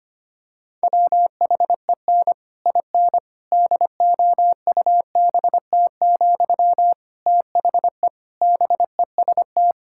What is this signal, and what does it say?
Morse code, heard as a keyed tone, about 25 wpm, 700 Hz. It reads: WHEN IN DOUBT, THE BEST
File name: practice / 25wpm / Difficult Sentences with Hints - 25wpm